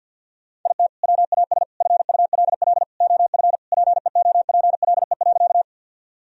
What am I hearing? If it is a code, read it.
Morse code, heard as a keyed tone, about 46 wpm, 700 Hz. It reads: IT WAS FULL OF PEOPLE.
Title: Sentences from Top 1000 Words - Review Part 3 - 50wpm